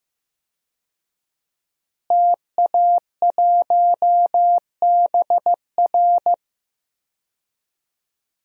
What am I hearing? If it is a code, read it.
Morse code, heard as a keyed tone, about 15 wpm, 700 Hz. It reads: TA1BR
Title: DXCC-100 Call Signs - 5 Characters - Repeated 2x - 15wpm